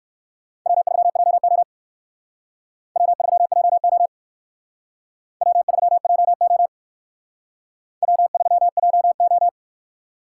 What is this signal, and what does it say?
Morse code, heard as a keyed tone, about 59 wpm, 700 Hz. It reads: W3JO W3JO W3JO W3JO